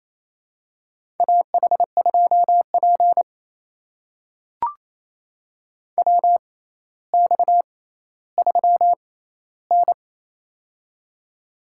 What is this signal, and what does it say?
Morse code, heard as a keyed tone, about 28 wpm, spaced as for 14 wpm, 700 Hz. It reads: AH2P WX3N
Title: USA Call Signs - 6x Character Spacing - 28wpm